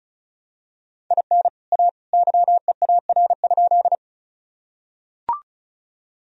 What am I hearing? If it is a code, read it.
Morse code, heard as a keyed tone, about 34 wpm, 700 Hz. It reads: IN A YEAR?
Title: Sentences from Top 1000 Words - Review Part 1 - 35wpm